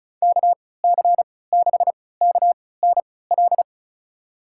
Morse code, heard as a keyed tone, about 34 wpm, spaced as for 20 wpm, 700 Hz. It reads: KC6KNL
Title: USA Call Signs - 6 Characters - 3x Character Spacing - 35wpm